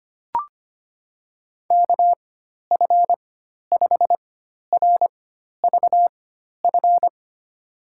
Morse code, heard as a keyed tone, about 25 wpm, spaced as for 12 wpm, 700 Hz. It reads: KF5RVF
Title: USA Call Signs - 4x Character Spacing - 25wpm